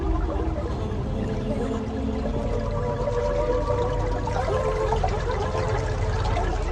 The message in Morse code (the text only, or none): none